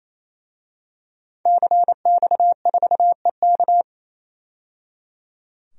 Morse code, heard as a keyed tone, about 28 wpm, 700 Hz. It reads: CX4EK
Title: DXCC-100 Call Signs - 5 Characters - 28wpm